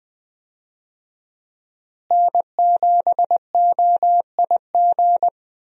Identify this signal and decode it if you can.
Morse code, heard as a keyed tone, about 20 wpm, 700 Hz. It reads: N7OIG
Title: USA Call Signs - 5 Characters - Repeated 3x - 20wpm